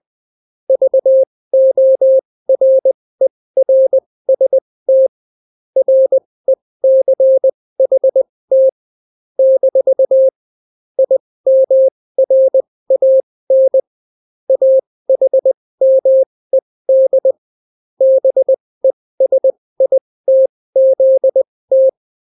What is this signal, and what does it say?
Morse code, heard as a keyed tone, about 20 wpm, spaced as for 16 wpm, 530 Hz. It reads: VORERST RECHT - IMRAN AHMED BESITZT